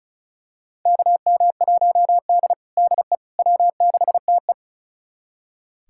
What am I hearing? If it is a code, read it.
Morse code, heard as a keyed tone, about 36 wpm, 700 Hz. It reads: KM1D DE W6TE
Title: USA Callsign DE Callsign - 4 Characters - 35wpm